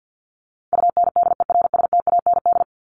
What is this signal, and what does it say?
Morse code, heard as a keyed tone, about 48 wpm, 700 Hz. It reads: UNDERSTAND
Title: Top 501-750 Words 50wpm